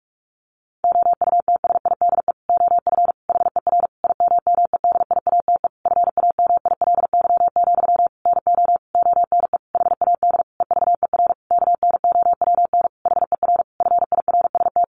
Morse code, heard as a keyed tone, about 44 wpm, 700 Hz. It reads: OUTSIDE OF HER IMMEDIATE FAMILY, NO ONE HAD EVER KNOWN HER FIRST
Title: Sentences from Top 1200 Words 45wpm